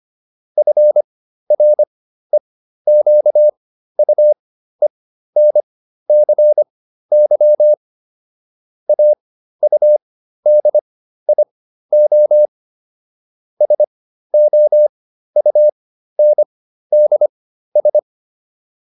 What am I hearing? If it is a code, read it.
Morse code, heard as a keyed tone, about 25 wpm, spaced as for 13 wpm, 600 Hz. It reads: FREQUENCY AUDIO SOUNDS